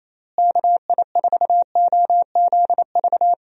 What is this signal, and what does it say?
Morse code, heard as a keyed tone, about 28 wpm, 700 Hz. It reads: KI4OZV
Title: USA Call Signs - 6 Characters - Repeated 3x - 28wpm